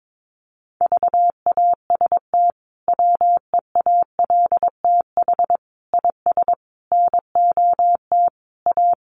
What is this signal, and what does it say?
Morse code, heard as a keyed tone, about 22 wpm, 700 Hz. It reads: VAST WEALTH IS NOT A
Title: Sentences from Top 1300 Words 22wpm